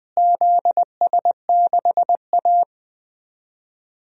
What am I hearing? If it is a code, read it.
Morse code, heard as a keyed tone, about 21 wpm, 700 Hz. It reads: ZS6A
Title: DXCC-100 Call Signs - 4 Characters - Repeated 3x - 20wpm